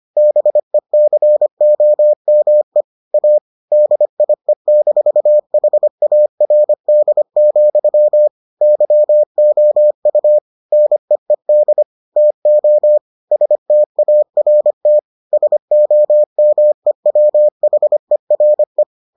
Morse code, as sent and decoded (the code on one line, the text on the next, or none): -... -.-. --- -- . / .- / -.. .. -....- .... .- .-. -.. --..-- / -.-- --- ..- / -. . -.. / - --- / ... - .- .-. - / ... --- -- . .-- .... .-. .
BECOME A DIE-HARD, YOU NEED TO START SOMEWHERE